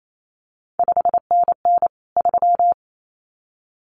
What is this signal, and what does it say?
Morse code, heard as a keyed tone, about 28 wpm, 700 Hz. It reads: 5NN 3
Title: CQ WW DX Contest Exchange Encore - 28wpm